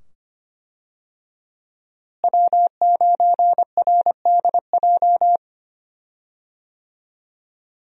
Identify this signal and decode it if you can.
Morse code, heard as a keyed tone, about 25 wpm, 700 Hz. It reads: W9RDJ
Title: USA Call Signs - 5 Characters - Repeated 2x - 25wpm